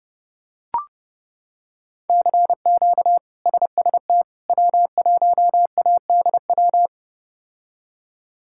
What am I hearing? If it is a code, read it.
Morse code, heard as a keyed tone, about 30 wpm, 700 Hz. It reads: CQ SST W1ADW
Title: K1USN SST QSO Practice - 30wpm